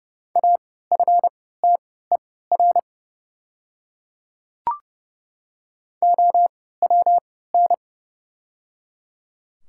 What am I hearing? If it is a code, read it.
Morse code, heard as a keyed tone, about 30 wpm, spaced as for 17 wpm, 700 Hz. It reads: AFTER OWN
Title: 3x Character Spacing - Top 300 Words - 30wpm